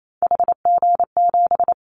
Morse code, heard as a keyed tone, about 28 wpm, 700 Hz. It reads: HG7
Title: Callsign Prefixes for Top 25 DXCC - Repeated 3x - 28wpm